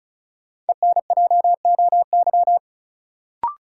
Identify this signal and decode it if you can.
Morse code, heard as a keyed tone, about 34 wpm, 700 Hz. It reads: ENJOY
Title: Sentences from Top 1300 Words - New Word Drill Alt - 35wpm